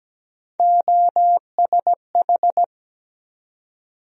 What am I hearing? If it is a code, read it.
Morse code, heard as a keyed tone, about 17 wpm, 700 Hz. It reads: OSH
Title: USA Call Sign Postfixes - 3 Letters 17wpm